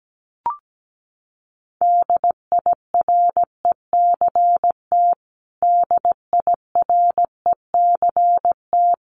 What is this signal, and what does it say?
Morse code, heard as a keyed tone, about 17 wpm, 700 Hz. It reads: DIRECT DIRECT
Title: Top 500 Words Repeated 17wpm